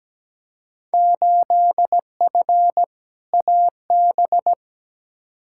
Morse code, heard as a keyed tone, about 17 wpm, 700 Hz. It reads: 8F AB
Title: Field Day Exchanges Encore - 17wpm